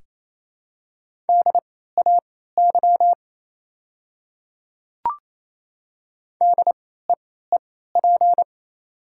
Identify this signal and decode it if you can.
Morse code, heard as a keyed tone, about 28 wpm, spaced as for 16 wpm, 700 Hz. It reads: DAY DEEP
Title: 3x Character Spacing - Top 500 Words - 28wpm